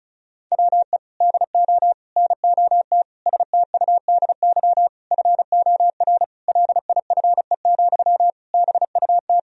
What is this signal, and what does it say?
Morse code, heard as a keyed tone, about 34 wpm, 700 Hz. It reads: WE DO NOT STUDY FOR LIFE, BUT
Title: Sentences from Top 1000 Words - Review Part 1 - 35wpm